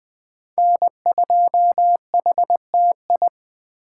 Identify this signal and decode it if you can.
Morse code, heard as a keyed tone, about 24 wpm, 700 Hz. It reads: N2HTI
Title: Speed Racing - USA Call Signs 20wpm